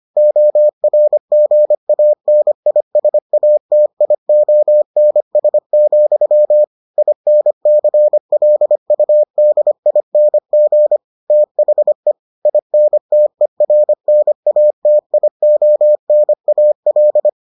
Morse code, heard as a keyed tone, about 25 wpm, 600 Hz. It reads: ORGANISATIONS, INCLUDING THE INTERNATIONAL